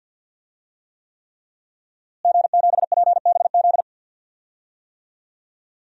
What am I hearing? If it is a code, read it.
Morse code, heard as a keyed tone, about 51 wpm, 700 Hz. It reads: G7PBZ
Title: DXCC-100 Call Signs - 5 Characters - Repeated 3x - 50wpm